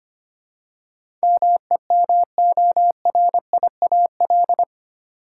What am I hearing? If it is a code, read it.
Morse code, heard as a keyed tone, about 25 wpm, 700 Hz. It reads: MEMORIAL
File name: Eight Letter Words - 25wpm